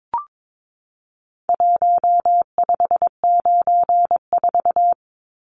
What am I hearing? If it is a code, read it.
Morse code, heard as a keyed tone, about 22 wpm, 700 Hz. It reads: 1594
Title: Numbers - Contest Style - 4-Digit 22wpm